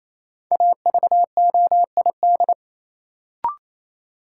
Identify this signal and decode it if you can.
Morse code, heard as a keyed tone, about 28 wpm, 700 Hz. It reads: AVOID
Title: Sentences from Top 1000 Words - Review Part 6 - New Word Drill - 28wpm